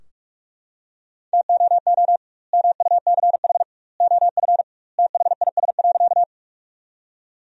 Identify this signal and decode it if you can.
Morse code, heard as a keyed tone, about 43 wpm, 700 Hz. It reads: TOO MUCH OF THIS.
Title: 2x Word-Spacing - Sentences from Top 200 Words - 45wpm